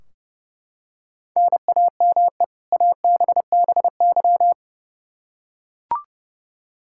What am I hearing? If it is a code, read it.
Morse code, heard as a keyed tone, about 31 wpm, 700 Hz. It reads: NAME ABBY
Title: Practice QSO - Exchange 1 - Beginning - 30wpm